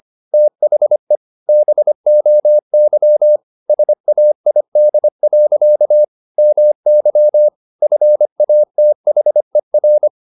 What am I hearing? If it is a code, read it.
Morse code, heard as a keyed tone, about 25 wpm, 600 Hz. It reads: THE BOY SAID. MY FATHER
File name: the old man and the sea part 1 25wpm